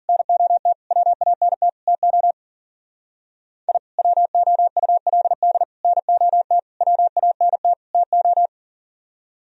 Morse code, heard as a keyed tone, about 38 wpm, 700 Hz. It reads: NOT WANT TO I WOULD NOT WANT TO
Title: Speed-Racing - Sets of 5 Words 40wpm